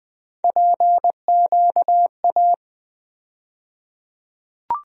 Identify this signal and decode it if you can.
Morse code, heard as a keyed tone, about 20 wpm, 700 Hz. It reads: PQA
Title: USA Call Sign Postfixes - 3 Letters 20wpm